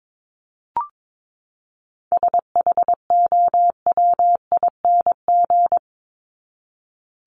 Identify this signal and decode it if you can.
Morse code, heard as a keyed tone, about 22 wpm, 700 Hz. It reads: SHOWING
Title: ING Suffix - Top 100 Words with Base Words 22wpm